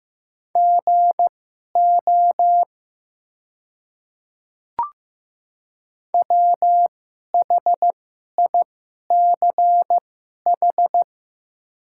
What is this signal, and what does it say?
Morse code, heard as a keyed tone, about 15 wpm, spaced as for 11 wpm, 700 Hz. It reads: GO WHICH
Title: 2x Character Spacing - Top 200 Words - 15wpm